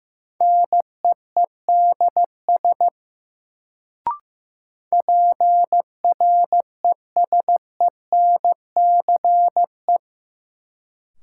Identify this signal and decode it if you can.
Morse code, heard as a keyed tone, about 15 wpm, 700 Hz. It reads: NEEDS PRESENCE
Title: Sets of 3 Words - New Word Drill - 15wpm